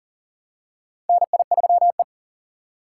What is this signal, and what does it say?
Morse code, heard as a keyed tone, about 40 wpm, 700 Hz. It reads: NI3E
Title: USA Call Signs - 4 Characters - 2x Character Spacing - 40wpm